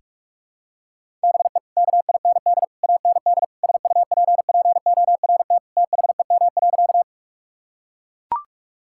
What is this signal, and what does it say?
Morse code, heard as a keyed tone, about 41 wpm, 700 Hz. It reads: BE KIND AND SUPPORT THEM.
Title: Sentences from Top 1000 Words - Review Part 5 - 45wpm